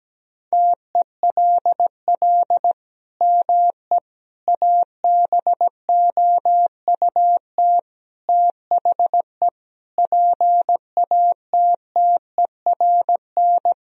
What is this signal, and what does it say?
Morse code, heard as a keyed tone, about 17 wpm, 700 Hz. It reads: TELL ME ABOUT THE PATTERN
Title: Sentences from Top 1000 Words - Review Part 2 - 17wpm